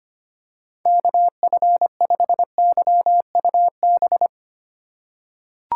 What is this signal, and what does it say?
Morse code, heard as a keyed tone, about 25 wpm, 700 Hz. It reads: KF5YUB
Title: K1USN SST QSO Practice - 25wpm